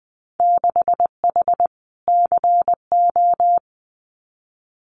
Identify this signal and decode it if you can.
Morse code, heard as a keyed tone, about 20 wpm, 700 Hz. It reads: 6H CO
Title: Winter Field Day Exchanges Encore 20wpm